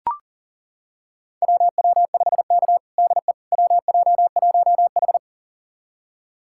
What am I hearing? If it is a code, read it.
Morse code, heard as a keyed tone, about 37 wpm, 700 Hz. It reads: WW5K DE WJ1H
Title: USA Callsign DE Callsign - 4 Characters - 40wpm